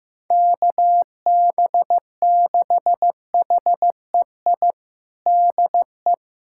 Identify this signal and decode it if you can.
Morse code, heard as a keyed tone, about 15 wpm, 700 Hz. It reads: KB6HEI DE
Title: USA Callsign DE Callsign - 6 Characters - 15wpm